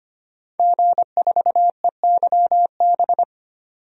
Morse code, heard as a keyed tone, about 25 wpm, 690 Hz. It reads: G4EYB
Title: DXCC-100 Call Signs - 5 Characters - 25wpm